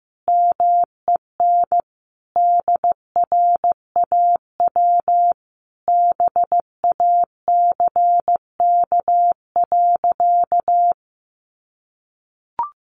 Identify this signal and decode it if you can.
Morse code, heard as a keyed tone, about 15 wpm, 700 Hz. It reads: MEN DRAW BACK.
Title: Sentences from Top 300 Words 15wpm